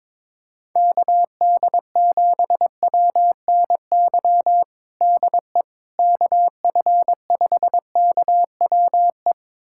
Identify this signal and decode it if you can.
Morse code, heard as a keyed tone, about 22 wpm, 700 Hz. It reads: KD7WNY DE KF5KWE
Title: USA Callsign DE Callsign - 6 Characters - 22wpm